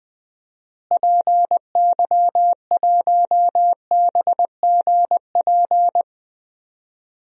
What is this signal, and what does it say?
Morse code, heard as a keyed tone, about 20 wpm, 700 Hz. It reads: PY1BGP
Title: DXCC-100 Call Signs - 6 Characters - Repeated 3x - 20wpm